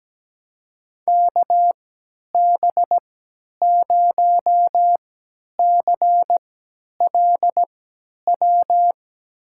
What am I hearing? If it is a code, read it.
Morse code, heard as a keyed tone, about 17 wpm, spaced as for 10 wpm, 700 Hz. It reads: KB0CLW
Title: USA Call Signs - 6 Characters - 3x Character Spacing - 17wpm